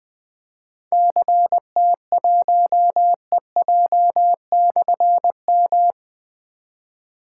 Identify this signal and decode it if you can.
Morse code, heard as a keyed tone, about 20 wpm, 700 Hz. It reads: CT1EJ/M
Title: Difficult Call Signs 20wpm